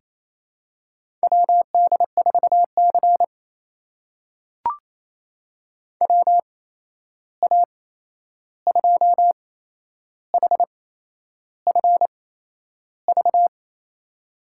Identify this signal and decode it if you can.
Morse code, heard as a keyed tone, about 28 wpm, spaced as for 11 wpm, 700 Hz. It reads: WD4C WA2HFV
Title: USA Call Signs - 8x Character Spacing - 28wpm